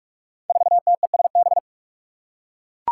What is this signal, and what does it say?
Morse code, heard as a keyed tone, about 44 wpm, 700 Hz. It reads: 4TESB
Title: Five Challenging Letter-Number Combinations 45wpm